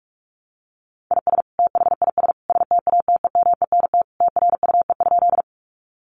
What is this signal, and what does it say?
Morse code, heard as a keyed tone, about 44 wpm, 700 Hz. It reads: IS THIS STATEMENT TRUE?
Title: Sentences from Top 1200 Words 45wpm